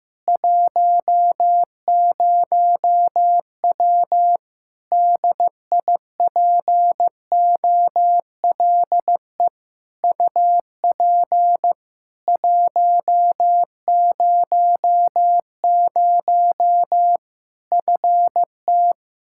Morse code, heard as a keyed tone, about 16 wpm, 700 Hz. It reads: 10W DIPOLE UP 100 FT
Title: Practice QSO - Exchange 2 - Info - 15wpm